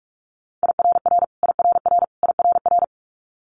Morse code, heard as a keyed tone, about 46 wpm, 700 Hz. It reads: IRR IRR IRR
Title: Multi-letter Phonemes Repeated 3x 45wpm